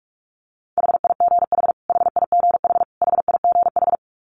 Morse code, heard as a keyed tone, about 44 wpm, 700 Hz. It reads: HIGH HIGH HIGH
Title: Top 200 Words Repeated 45wpm